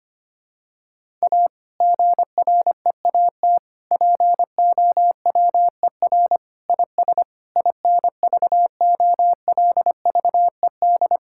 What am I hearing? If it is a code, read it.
Morse code, heard as a keyed tone, about 25 wpm, 700 Hz. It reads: A GREAT POWER IS INVOLVED
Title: Sentences from Top 1000 Words - Review Part 5 - 25wpm